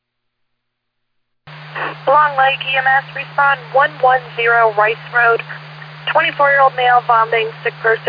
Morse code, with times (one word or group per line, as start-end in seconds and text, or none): none